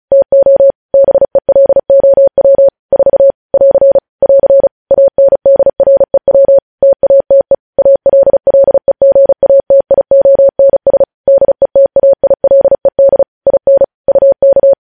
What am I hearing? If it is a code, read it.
Morse code, heard as a keyed tone, about 36 wpm, 550 Hz. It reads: TO BELOW 4 <AR> <AR> ANDREW TATE ALLEGATIONS DETAILED IN UK
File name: podcast / News headlines-1x-35x35